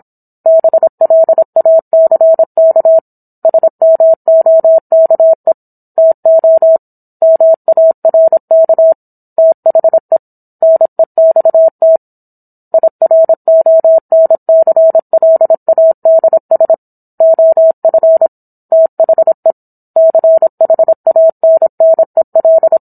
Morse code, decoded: BLACK SMOKE TO MARK THE NEXT IRONCLADS OF THE CHANNEL

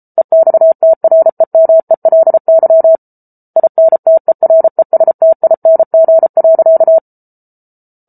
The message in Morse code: EXTREMELY INTERESTING.

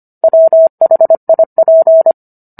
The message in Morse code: WHIP

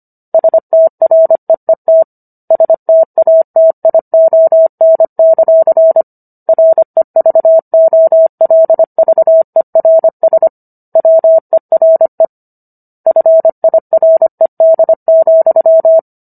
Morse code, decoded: STREET STATION; REVOLVERS WERE FIRED,

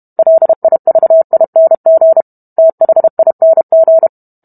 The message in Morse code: LIVING THING

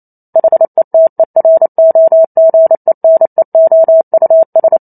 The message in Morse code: HETEROGENEOUS

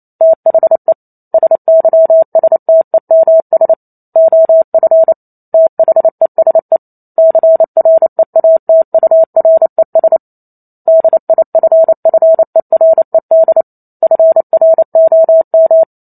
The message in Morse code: THE SYSTEMS OF THESE CREATURES DIFFERED FROM

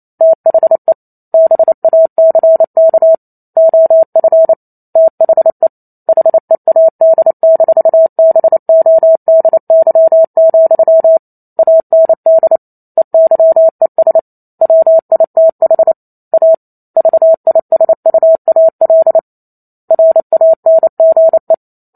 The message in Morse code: THE BACK OF THE HEAD-BODY, AND EYES WITH A VISUAL RANGE

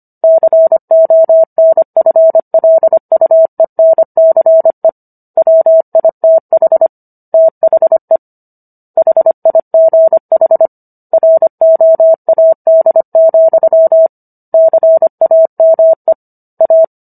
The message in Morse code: CONFLUENCE WITH THE HIGH ROAD, CAME A

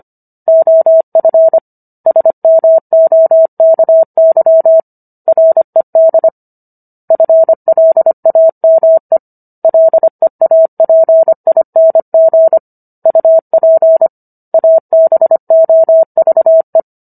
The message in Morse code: OF SMOKY RED FLAME LEAPING UP ABOVE

